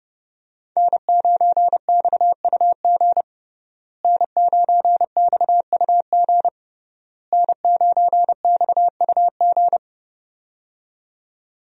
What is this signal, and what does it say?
Morse code, heard as a keyed tone, about 29 wpm, 700 Hz. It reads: N9XUG N9XUG N9XUG
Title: USA Call Signs - 5 Characters - Repeated 3x - 30wpm